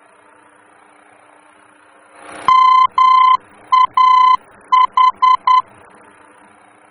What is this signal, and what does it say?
Morse code, heard as a keyed tone, about 10 wpm, 1020 Hz. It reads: MAH
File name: germany / mx / MAH-D-VOR